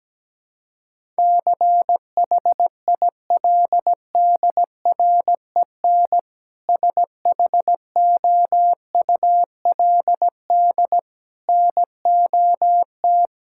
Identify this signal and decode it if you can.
Morse code, heard as a keyed tone, about 17 wpm, 700 Hz. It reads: CHILDREN SHOULD NOT
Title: Sentences from Top 1000 Words - Review Part 5 - 17wpm